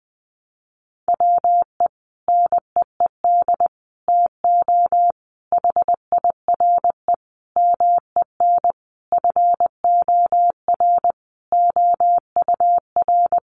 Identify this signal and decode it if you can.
Morse code, heard as a keyed tone, about 20 wpm, 700 Hz. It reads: WE NEED TO HIRE MEN FOR OUR
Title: Sentences from Top 1400 Words 20wpm